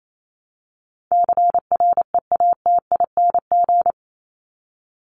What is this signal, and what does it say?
Morse code, heard as a keyed tone, about 28 wpm, 700 Hz. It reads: CREATING
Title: ING Suffix - Top 100 Words with Base Words 28wpm